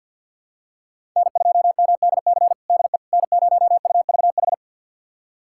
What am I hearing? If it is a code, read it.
Morse code, heard as a keyed tone, about 49 wpm, 700 Hz. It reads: N2MDC DE N0UVH
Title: USA Callsign DE Callsign - 5 Characters - 50wpm